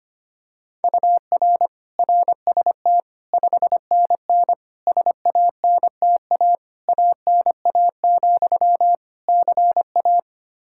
Morse code, heard as a keyed tone, about 25 wpm, 700 Hz. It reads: UR RST 5NN SANTA ANA, CA
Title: Practice QSO - Exchange 1 - Beginning - 25wpm